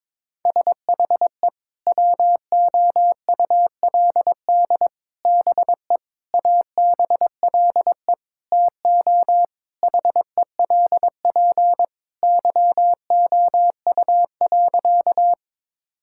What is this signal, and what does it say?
Morse code, heard as a keyed tone, about 21 wpm, 700 Hz. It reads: SHE WOULD BE ABLE TO HELP YOU.
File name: practice / 22wpm / Sentences with 4-Word Phrases - 22wpm